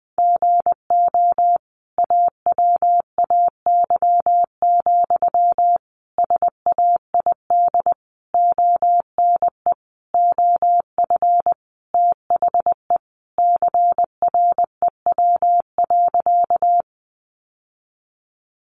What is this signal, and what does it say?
Morse code, heard as a keyed tone, about 20 wpm, 700 Hz. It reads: GO AWAY, SAID ONE OF THE CREW.
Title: Sentences from Top 1500 Words 20wpm